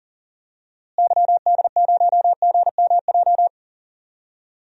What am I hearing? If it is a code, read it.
Morse code, heard as a keyed tone, about 40 wpm, 700 Hz. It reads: YD0GMJ
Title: DXCC-100 Call Signs - 6 Characters - Repeated 2x - 40wpm